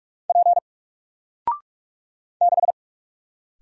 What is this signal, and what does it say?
Morse code, heard as a keyed tone, about 46 wpm, 700 Hz. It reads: P 6
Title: Single Letter-Number 45wpm